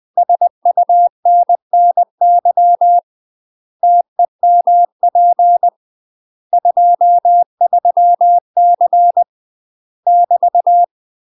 Morse code, heard as a keyed tone, about 20 wpm, 700 Hz. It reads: SUNNY TEMP 23C <BT>
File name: UK MASTER QSOs TRAINING / 20WPM / UK MASTER QSOs 0062